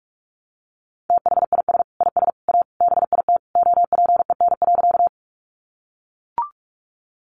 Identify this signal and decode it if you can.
Morse code, heard as a keyed tone, about 44 wpm, 700 Hz. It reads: THIS IS A BIT OPEN.
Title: Sentences from Top 900 Words 45wpm